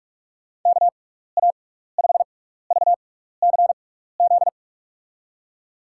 Morse code, heard as a keyed tone, about 44 wpm, spaced as for 15 wpm, 700 Hz. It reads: KA5VCZ
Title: USA Call Signs - 6 Characters - 6x Character Spacing - 45wpm